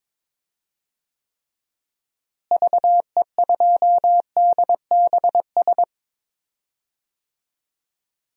Morse code, heard as a keyed tone, about 22 wpm, 700 Hz. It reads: VE2DBS
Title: DXCC-100 Call Signs - 6 Characters - Repeated 2x - 22wpm